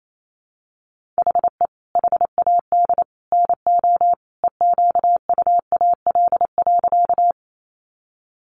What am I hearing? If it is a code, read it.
Morse code, heard as a keyed tone, about 28 wpm, 700 Hz. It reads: HE HAD NO EQUAL.
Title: Sentences from Top 1400 Words 28wpm